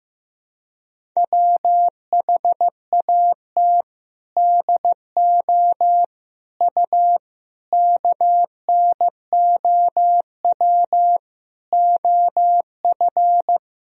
Morse code, heard as a keyed tone, about 15 wpm, 700 Hz. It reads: WHAT DO U KNOW OF
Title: Sentences from Top 100 Words and Abbreviations 15wpm